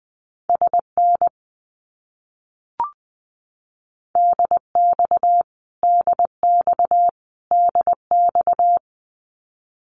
Morse code, heard as a keyed tone, about 20 wpm, 700 Hz. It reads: SN DX DX DX
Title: Abbreviations Repeated 3x - 20wpm